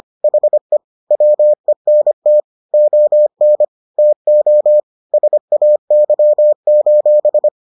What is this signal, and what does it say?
Morse code, heard as a keyed tone, about 25 wpm, 600 Hz. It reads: HE WENT ON TO SAY: